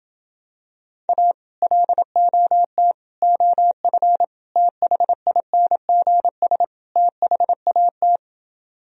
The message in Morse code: A LOT OF THINGS THAT

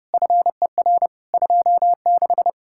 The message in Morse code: FER 26

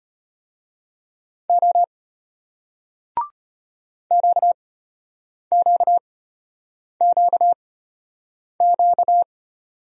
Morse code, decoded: O QQQQ

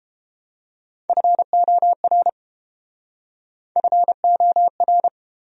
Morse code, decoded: FOR FOR

